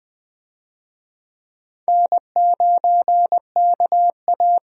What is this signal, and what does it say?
Morse code, heard as a keyed tone, about 20 wpm, 700 Hz. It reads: N9KA
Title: USA Call Signs - 4 Characters - 20wpm